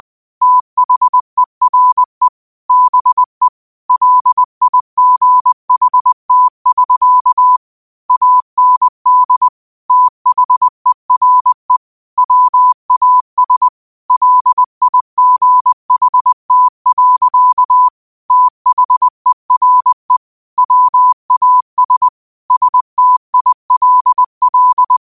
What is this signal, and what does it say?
Morse code, heard as a keyed tone, about 20 wpm, 1000 Hz. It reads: THERE BE LIGHT<SK> AND THERE WAS LIGHT. THERE WAS STILL
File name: 20wpm